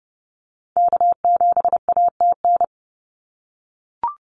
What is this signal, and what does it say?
Morse code, heard as a keyed tone, about 30 wpm, 700 Hz. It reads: K7ATN
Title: NA SOTA Activator Callsigns 30wpm